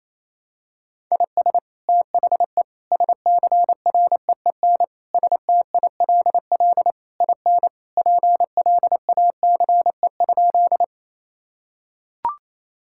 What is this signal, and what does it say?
Morse code, heard as a keyed tone, about 28 wpm, 700 Hz. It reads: IS THE SCREEN STILL IN PLACE?
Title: Sentences from Top 1000 Words - Review Part 6 - 28wpm